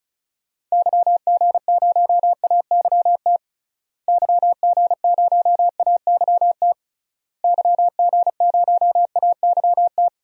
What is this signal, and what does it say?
Morse code, heard as a keyed tone, about 34 wpm, 700 Hz. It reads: YG0AYT YG0AYT YG0AYT
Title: DXCC-100 Call Signs - 6 Characters - Repeated 3x - 35wpm